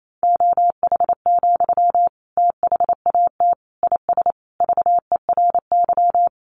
Morse code, heard as a keyed tone, about 28 wpm, 700 Hz. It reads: OH, THAT IS VERY
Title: Sentences from Top 500 Words 28wpm